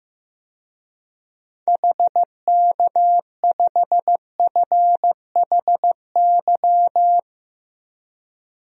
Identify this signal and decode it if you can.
Morse code, heard as a keyed tone, about 16 wpm, 700 Hz. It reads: HK5FHY
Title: DXCC-100 Call Signs - 6 Characters - Repeated 2x - 15wpm